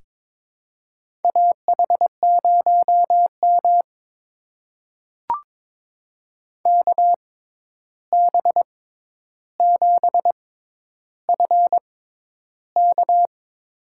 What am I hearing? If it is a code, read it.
Morse code, heard as a keyed tone, about 22 wpm, spaced as for 10 wpm, 700 Hz. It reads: AH0M KB7FK
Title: USA Call Signs - 6x Character Spacing - 22wpm